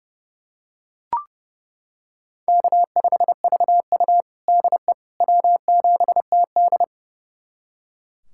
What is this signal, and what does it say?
Morse code, heard as a keyed tone, about 32 wpm, 700 Hz. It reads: K5VU DE W7TD
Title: USA Callsign DE Callsign - 4 Characters - 30wpm